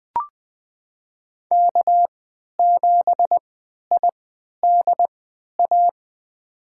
Morse code, heard as a keyed tone, about 20 wpm, spaced as for 11 wpm, 700 Hz. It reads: K7IDA
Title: USA Call Signs - 5 Characters - 3x Character Spacing - 20wpm